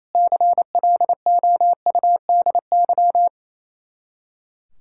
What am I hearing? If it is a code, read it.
Morse code, heard as a keyed tone, about 34 wpm, 700 Hz. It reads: CLOUDY
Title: Speed-Racing Common QSO Phrases Encore - 28wpm